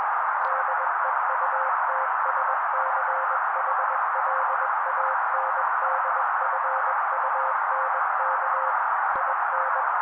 Morse code, heard as a keyed tone, about 21 wpm, 530 Hz. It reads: DEUTSCHLANDFUNK IN
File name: daily / 20wpm noise